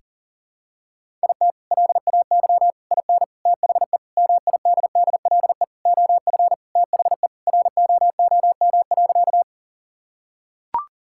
Morse code, heard as a keyed tone, about 37 wpm, 700 Hz. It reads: IT LAY IN THE MIDDLE OF THE ROOM.
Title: Sentences from Top 1000 Words - Review Part 6 - 40wpm